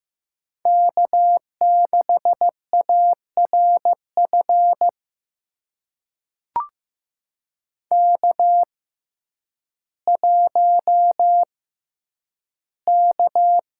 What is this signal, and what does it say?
Morse code, heard as a keyed tone, about 15 wpm, spaced as for 9 wpm, 700 Hz. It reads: K6ARF K1K
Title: USA Call Signs - 5 Characters - 6x Character Spacing - 15wpm